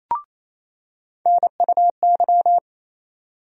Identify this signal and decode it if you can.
Morse code, heard as a keyed tone, about 28 wpm, 700 Hz. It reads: NUY
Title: USA Call Sign Postfixes - 3 Letters 28wpm